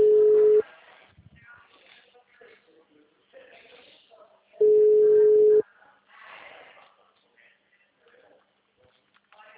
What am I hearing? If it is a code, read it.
Ringback tone (Europe, 425 Hz), with speech.
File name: out- -2035- - - .9